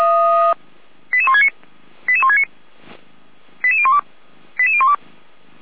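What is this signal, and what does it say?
Five-tone selective call (ZVEI), heard as quick runs of five tones, tones of about 70 ms. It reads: 88169 88169 88012 88012
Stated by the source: ZVEI five-tone call received on SDR.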